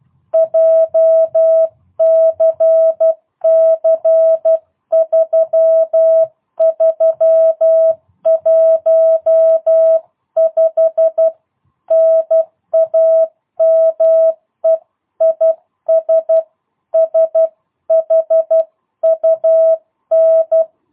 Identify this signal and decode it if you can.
Morse code, heard as a keyed tone, about 12 wpm, 650 Hz. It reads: JCC3315 NAME IS SHUN
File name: jh8cbh / 2022cw